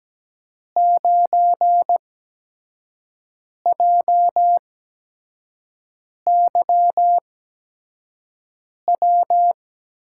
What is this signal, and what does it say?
Morse code, heard as a keyed tone, about 17 wpm, spaced as for 5 wpm, 700 Hz. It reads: 9JYW
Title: USA Call Signs - 6 Characters - 8x Character Spacing - 17wpm